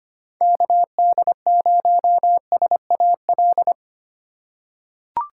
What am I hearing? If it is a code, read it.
Morse code, heard as a keyed tone, about 25 wpm, 700 Hz. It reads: KD0SAL
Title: USA Call Signs - 6 Characters - 2x Character Spacing - 25wpm